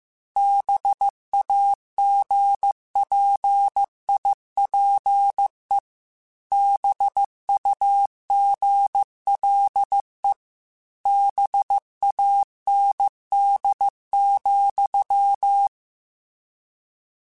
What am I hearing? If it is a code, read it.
Morse code, heard as a keyed tone, about 15 wpm, 800 Hz. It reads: BAGPIPE BUGLE BAND,